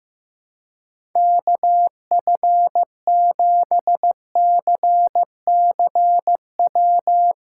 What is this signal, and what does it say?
Morse code, heard as a keyed tone, about 15 wpm, 700 Hz. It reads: KF7CCW
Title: USA Callsign DE Callsign - 6 Characters - 15wpm